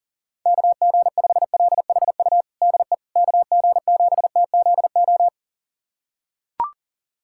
Morse code, heard as a keyed tone, about 40 wpm, 700 Hz. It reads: KG5LHU DE KG7TZO
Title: USA Callsign DE Callsign - 6 Characters - 40wpm